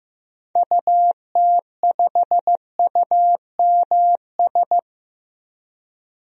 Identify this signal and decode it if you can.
Morse code, heard as a keyed tone, about 15 wpm, 700 Hz. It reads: UT5UMS
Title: DXCC-100 Call Signs - 6 Characters - 15wpm